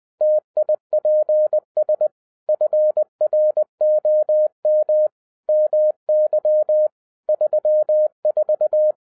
Morse code, heard as a keyed tone, about 20 wpm, 610 Hz. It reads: TIPS FROM MY 34
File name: 2011-10-mp3 / n13-TIPS FROM MY 34 YEARS OF EXPERIENCE 20wpm